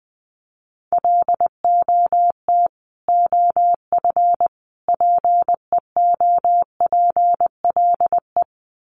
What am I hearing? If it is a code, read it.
Morse code, heard as a keyed tone, about 19 wpm, 700 Hz. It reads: LOT OF PEOPLE
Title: Sets of 3 Words 20wpm